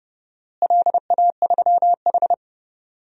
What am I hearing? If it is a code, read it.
Morse code, heard as a keyed tone, about 30 wpm, 700 Hz. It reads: LA3H
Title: DXCC-100 Call Signs - 4 Characters - Repeated 3x - 30wpm